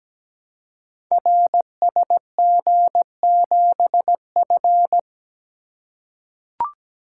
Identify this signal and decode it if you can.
Morse code, heard as a keyed tone, about 17 wpm, 700 Hz. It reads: RSG7F
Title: Code Groups - 5 Letter:Numbers - 17wpm